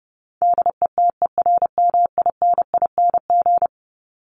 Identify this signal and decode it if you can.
Morse code, heard as a keyed tone, about 30 wpm, 700 Hz. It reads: DETERMINING
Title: ING Suffix - Top 100 Words with Base Words 30wpm